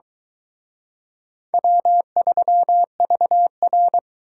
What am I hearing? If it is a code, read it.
Morse code, heard as a keyed tone, about 22 wpm, 700 Hz. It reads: W3VR